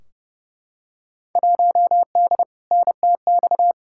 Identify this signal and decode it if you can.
Morse code, heard as a keyed tone, about 29 wpm, 700 Hz. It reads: 1D NTX
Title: Field Day Exchanges - 30wpm